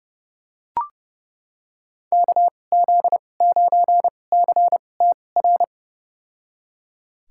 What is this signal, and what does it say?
Morse code, heard as a keyed tone, about 32 wpm, spaced as for 22 wpm, 700 Hz. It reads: KZ9CTR
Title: USA Call Signs - 6 Characters - 2x Character Spacing - 30wpm